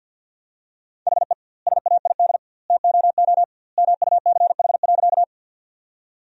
Morse code, heard as a keyed tone, about 50 wpm, 700 Hz. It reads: HE SAID TOO MUCH.
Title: 2x Word-Spacing - Sentences from Top 200 Words - 50wpm